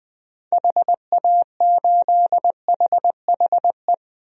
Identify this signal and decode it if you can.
Morse code, heard as a keyed tone, about 20 wpm, 700 Hz. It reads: HA8HHE
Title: DXCC-100 Call Signs - 6 Characters - Repeated 2x - 20wpm